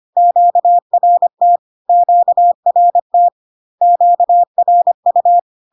Morse code, heard as a keyed tone, about 25 wpm, 700 Hz. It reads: QRT QRT QRU
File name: Q-codes most common repeated 3 times 25 25wpm